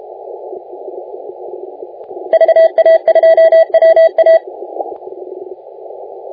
Morse code, heard as a keyed tone, about 32 wpm, 650 Hz. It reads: VA2WA
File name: la4c / sac-cw-2018 / CONTEST AUDIO 1600Z